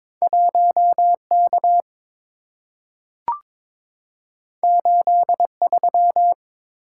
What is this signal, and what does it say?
Morse code, heard as a keyed tone, about 22 wpm, 700 Hz. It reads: 1K 83
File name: practice / 22wpm / Two Letter-Number Combinations - 22wpm